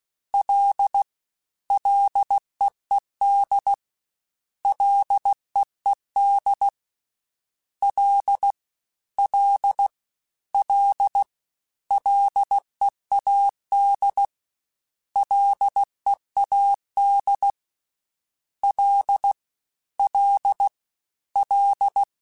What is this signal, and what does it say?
Morse code, heard as a keyed tone, about 16 wpm, 800 Hz. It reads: L LEED LEED L L L LEAD LEAD L L L
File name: F and L / F and L mp3 / F and L Prefixes 3x2 Twice @15